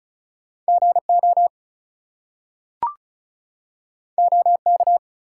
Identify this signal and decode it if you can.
Morse code, heard as a keyed tone, about 36 wpm, 700 Hz. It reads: GO OK
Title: Two Letter Words - 35wpm